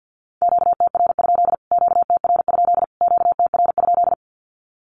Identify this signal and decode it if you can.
Morse code, heard as a keyed tone, about 50 wpm, 700 Hz. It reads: QTR? QTR? QTR?
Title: Q-Codes Repeated 3x 50wpm